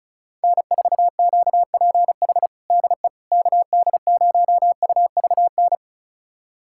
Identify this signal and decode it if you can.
Morse code, heard as a keyed tone, about 36 wpm, 700 Hz. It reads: N4QPH DE KD0UVN
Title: USA Callsign DE Callsign 35wpm